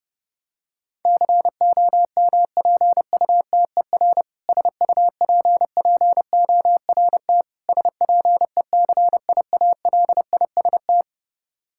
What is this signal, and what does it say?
Morse code, heard as a keyed tone, about 30 wpm, 700 Hz. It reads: COMPUTER SUPPORT SPECIALIST
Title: Practice QSO - Exchange 3 - More Info - 30wpm